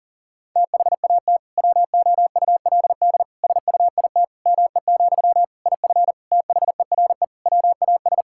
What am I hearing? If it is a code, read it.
Morse code, heard as a keyed tone, about 37 wpm, 700 Hz. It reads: THAT WOULD SUIT ME, IF THERE WAS
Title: Sentences from Top 1600 Words 40wpm